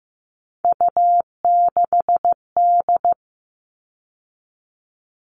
Morse code, heard as a keyed tone, about 15 wpm, 700 Hz. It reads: U6D
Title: All Cut Numbers - 3-Digit 15wpm